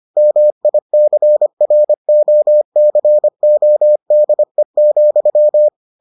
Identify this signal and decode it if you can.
Morse code, heard as a keyed tone, about 25 wpm, 600 Hz. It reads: MICROCODE,